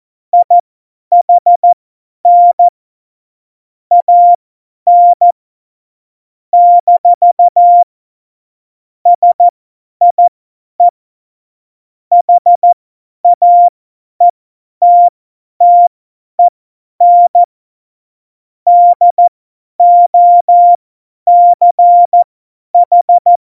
Morse code, heard as a keyed tone, about 14 wpm, spaced as for 10 wpm, 700 Hz. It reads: IHN AN - SIE HAETTEN DOCH